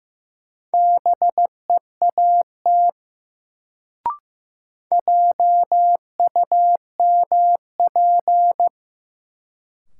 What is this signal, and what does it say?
Morse code, heard as a keyed tone, about 15 wpm, 700 Hz. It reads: BEAT JUMP
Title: Sentences from Top 1300 Words - New Word Drill - 15wpm